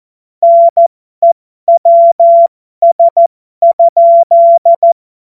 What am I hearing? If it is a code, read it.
Morse code, heard as a keyed tone, about 14 wpm, spaced as for 12 wpm, 680 Hz. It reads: NEWS?